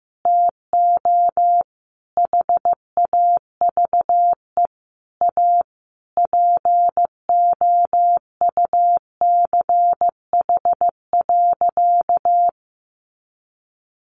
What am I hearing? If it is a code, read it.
Morse code, heard as a keyed tone, about 14 wpm, 700 Hz. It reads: TO HAVE A POUCH.
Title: Easy Phrases 15wpm